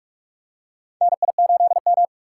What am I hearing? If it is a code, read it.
Morse code, heard as a keyed tone, about 43 wpm, 700 Hz. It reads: NI8M
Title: USA Call Signs - 4 Characters - 6x Character Spacing - 45wpm